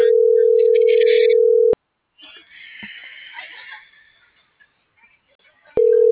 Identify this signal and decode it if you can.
Ringback tone (North America), 440 + 480 Hz.